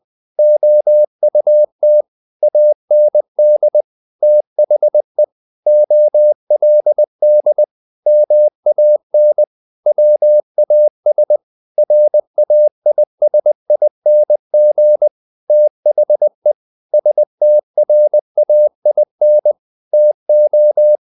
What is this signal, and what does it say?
Morse code, heard as a keyed tone, about 20 wpm, 600 Hz. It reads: OUT AND THE OLD MAN WAS RAISING THE STRAIN TO
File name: the old man and the sea part 2 20wpm